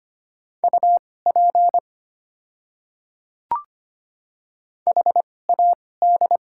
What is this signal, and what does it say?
Morse code, heard as a keyed tone, about 25 wpm, spaced as for 18 wpm, 690 Hz. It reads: UP HAD
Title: 2x Character Spacing - Top 100 Words - 25wpm